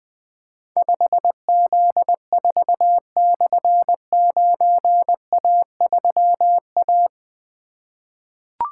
Morse code, heard as a keyed tone, about 20 wpm, 700 Hz. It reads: 5Z4/9A3A
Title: Difficult Call Signs 20wpm